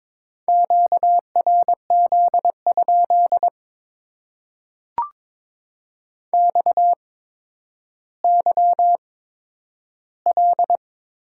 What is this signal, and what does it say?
Morse code, heard as a keyed tone, about 22 wpm, spaced as for 11 wpm, 700 Hz. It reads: QRZ? XYL
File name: Top 90 QSO Elements - 8x Character Spacing - 22wpm